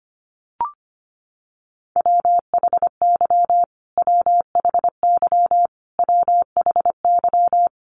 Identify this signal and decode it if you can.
Morse code, heard as a keyed tone, about 25 wpm, 700 Hz. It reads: WHY WHY WHY
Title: Top 200 Words Repeated 25wpm